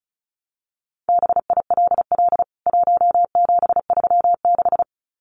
Morse code, heard as a keyed tone, about 34 wpm, 700 Hz. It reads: BILL 1736
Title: CWT Exchange 35wpm